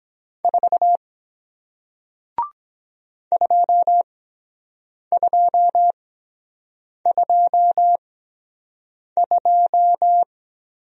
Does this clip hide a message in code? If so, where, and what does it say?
Yes: Morse code from 0.4 to 10.2 s, heard as a keyed tone, about 22 wpm, spaced as for 7 wpm, 700 Hz. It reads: 4 2222